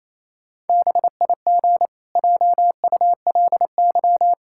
Morse code, heard as a keyed tone, about 28 wpm, 700 Hz. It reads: BIG JULY
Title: Pangram Challenge v2 - 28wpm